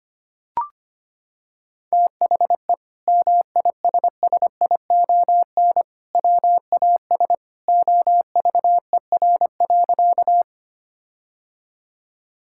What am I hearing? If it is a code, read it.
Morse code, heard as a keyed tone, about 25 wpm, 700 Hz. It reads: THE MISSION WAS OVER.
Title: Sentences from Top 1600 Words 25wpm